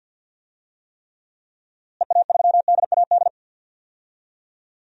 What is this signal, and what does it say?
Morse code, heard as a keyed tone, about 49 wpm, 700 Hz. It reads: EA3DAD
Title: DXCC-100 Call Signs - 6 Characters - Repeated 3x - 50wpm